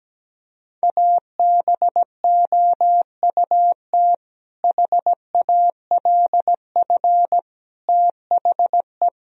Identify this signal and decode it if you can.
Morse code, heard as a keyed tone, about 17 wpm, 700 Hz. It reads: ABOUT HALF THE
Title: Sentences from Top 1000 Words - Review Part 1 - 17wpm